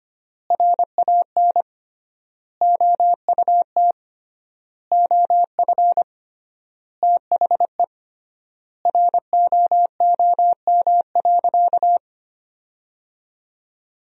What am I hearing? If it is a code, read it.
Morse code, heard as a keyed tone, about 25 wpm, 700 Hz. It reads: RAN OUT OF THE ROOM.
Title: 3x Word-Spacing - Sentences from Top 500 Words - 25wpm